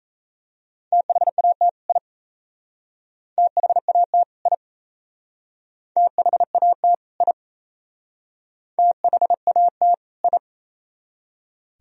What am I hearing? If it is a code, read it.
Morse code, heard as a keyed tone, about 33 wpm, 690 Hz. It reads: THAT I THAT I THAT I THAT I
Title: Speed-Racing - Sets of 2 Words 28wpm